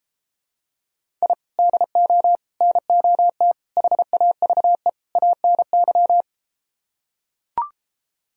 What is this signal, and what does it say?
Morse code, heard as a keyed tone, about 33 wpm, 700 Hz. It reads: I DO NOT HAVE ANY